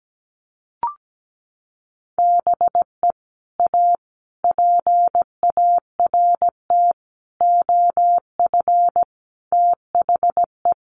Morse code, heard as a keyed tone, about 17 wpm, 700 Hz. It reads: BE A PART OF THE
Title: Sets of 5 Words Encore 17wpm